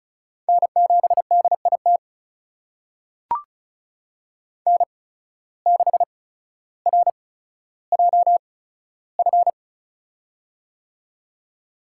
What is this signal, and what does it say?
Morse code, heard as a keyed tone, about 34 wpm, spaced as for 15 wpm, 700 Hz. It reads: N7DIT N6RJF
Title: USA Call Signs - 5 Characters - 8x Character Spacing - 35wpm